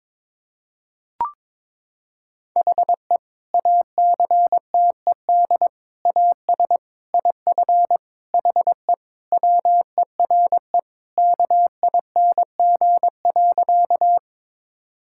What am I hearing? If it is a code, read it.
Morse code, heard as a keyed tone, about 22 wpm, 700 Hz. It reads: HE ACTED AS IF HE WERE KING.
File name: practice / 22wpm / Sentences with 4-Word Phrases - 22wpm